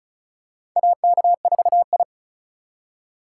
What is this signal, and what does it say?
Morse code, heard as a keyed tone, about 35 wpm, 700 Hz. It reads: AK4I
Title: USA Call Signs - 4 Characters - 3x Character Spacing - 35wpm